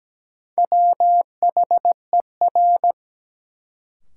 Morse code, heard as a keyed tone, about 17 wpm, 700 Hz. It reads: WHER
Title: Sentences from Top 1500 Words - New Word Drill - 17wpm